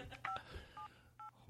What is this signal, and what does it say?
DTMF, heard as telephone touch tones, tones of about 110 ms. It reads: 578